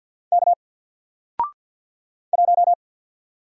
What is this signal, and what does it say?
Morse code, heard as a keyed tone, about 50 wpm, 700 Hz. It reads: K 1
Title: Single Letter-Number Alt 50wpm